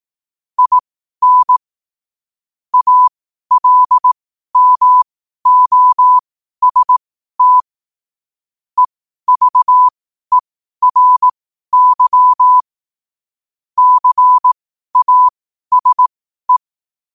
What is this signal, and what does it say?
Morse code, heard as a keyed tone, about 18 wpm, spaced as for 12 wpm, 1000 Hz. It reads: IN ALMOST EVERY CASE
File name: qotdfiles / 12wpm